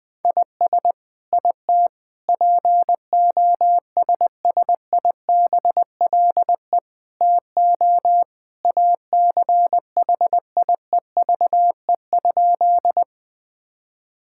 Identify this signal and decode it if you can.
Morse code, heard as a keyed tone, about 20 wpm, 700 Hz. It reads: IS IT POSSIBLE TO ACHIEVE?
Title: Sentences from Top 1000 Words - Review Part 6 - 20wpm